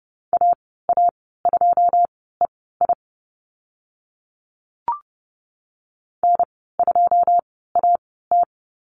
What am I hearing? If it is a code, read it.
Morse code, heard as a keyed tone, about 30 wpm, spaced as for 17 wpm, 700 Hz. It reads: AA2EI N2AT
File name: Lesson 14g - Number 2 - Call Signs 30wpm